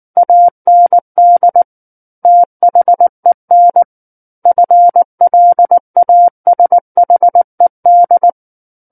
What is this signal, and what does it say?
Morse code, heard as a keyed tone, about 19 wpm, 700 Hz. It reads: AND THEN FLASHED
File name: Book One / 10-Chapter 10 - 19 WPM